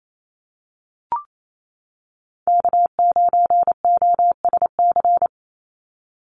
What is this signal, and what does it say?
Morse code, heard as a keyed tone, about 28 wpm, 700 Hz. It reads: K9OSC
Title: NA SOTA Chaser Callsigns 28wpm